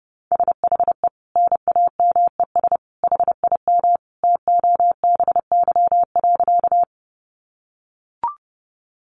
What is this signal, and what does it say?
Morse code, heard as a keyed tone, about 30 wpm, 700 Hz. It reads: SHE NAMES HIM TOBY.
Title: Easy Phrases 30wpm